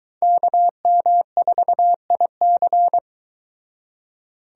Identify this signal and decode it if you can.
Morse code, heard as a keyed tone, about 22 wpm, 700 Hz. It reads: KM4IC